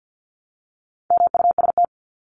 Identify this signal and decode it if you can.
Morse code, heard as a keyed tone, about 50 wpm, 700 Hz. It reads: MUST
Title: Top 500 Words 50wpm